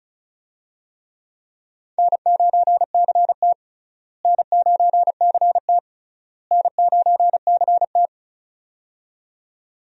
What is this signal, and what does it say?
Morse code, heard as a keyed tone, about 34 wpm, 700 Hz. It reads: N9CT N9CT N9CT
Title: USA Call Signs - 4 Characters - Repeated 3x - 35wpm